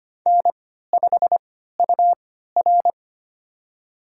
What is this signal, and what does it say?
Morse code, heard as a keyed tone, about 25 wpm, spaced as for 14 wpm, 700 Hz. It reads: N5UR
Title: USA Call Signs - 4 Characters - 3x Character Spacing - 25wpm